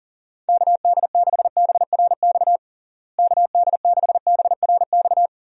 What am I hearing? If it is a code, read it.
Morse code, heard as a keyed tone, about 40 wpm, 700 Hz. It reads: KD6BRX KD6BRX
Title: USA Call Signs - 6 Characters - Repeated 3x - 40wpm